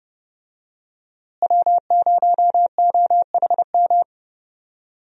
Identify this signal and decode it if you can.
Morse code, heard as a keyed tone, about 30 wpm, 700 Hz. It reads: W0OHM
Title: USA Call Signs - 5 Characters - Repeated 2x - 30wpm